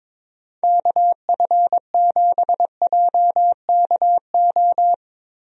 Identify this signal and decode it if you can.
Morse code, heard as a keyed tone, about 21 wpm, 700 Hz. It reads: KF7JKO
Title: USA Call Signs - 6 Characters - Repeated 2x - 22wpm